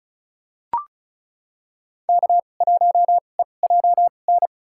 Morse code, heard as a keyed tone, about 36 wpm, spaced as for 25 wpm, 700 Hz. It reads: K1EJN
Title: USA Call Signs - 5 Characters - 2x Character Spacing - 35wpm